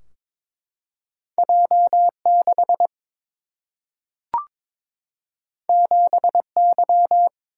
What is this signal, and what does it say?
Morse code, heard as a keyed tone, about 22 wpm, 700 Hz. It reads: J 6 7 Y
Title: Single-Letter Single-Number 22wpm